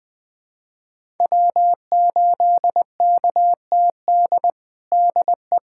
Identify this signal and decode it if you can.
Morse code, heard as a keyed tone, about 20 wpm, 700 Hz. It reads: W8KTD DE
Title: USA Callsign DE Callsign 20wpm